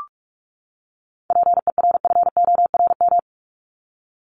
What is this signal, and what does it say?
Morse code, heard as a keyed tone, about 44 wpm, 700 Hz. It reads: PERFORM
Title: ED Suffix - Top 100 Words with Base Words 45wpm